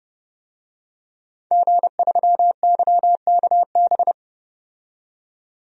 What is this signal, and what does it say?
Morse code, heard as a keyed tone, about 30 wpm, 700 Hz. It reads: G3YKB
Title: DXCC-100 Call Signs - 5 Characters - 30wpm